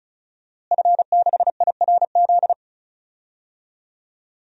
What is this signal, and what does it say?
Morse code, heard as a keyed tone, about 34 wpm, 700 Hz. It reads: F6IRZ
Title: DXCC-100 Call Signs - 5 Characters - Repeated 2x - 35wpm